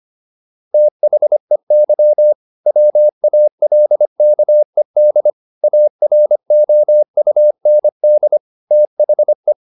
Morse code, heard as a keyed tone, about 25 wpm, 600 Hz. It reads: THEY WALKED AROUND THE